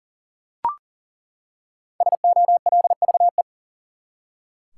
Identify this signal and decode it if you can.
Morse code, heard as a keyed tone, about 40 wpm, 700 Hz. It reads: SOLVE
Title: Five Letter Words - 40wpm